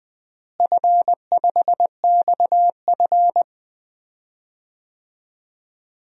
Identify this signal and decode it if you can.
Morse code, heard as a keyed tone, about 20 wpm, 700 Hz. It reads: F5XF
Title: DXCC-100 Call Signs - 4 Characters - Repeated 3x - 20wpm